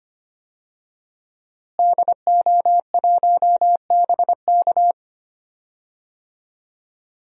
Morse code, heard as a keyed tone, about 25 wpm, 700 Hz. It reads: DO1BK
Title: DXCC-100 Call Signs - 5 Characters - Repeated 3x - 25wpm